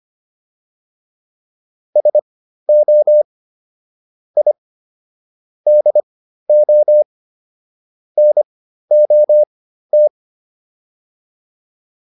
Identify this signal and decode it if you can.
Morse code, heard as a keyed tone, about 25 wpm, spaced as for 13 wpm, 600 Hz. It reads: SO I DO NOT